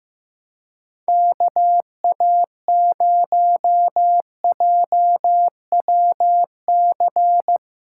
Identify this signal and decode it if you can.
Morse code, heard as a keyed tone, about 15 wpm, 700 Hz. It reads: KA0JWC
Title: USA Callsign DE Callsign - 6 Characters - 15wpm